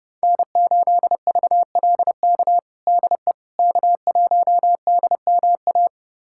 Morse code, heard as a keyed tone, about 30 wpm, 700 Hz. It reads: N8VLK DE K1DMA
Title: USA Callsign DE Callsign - 5 Characters - 30wpm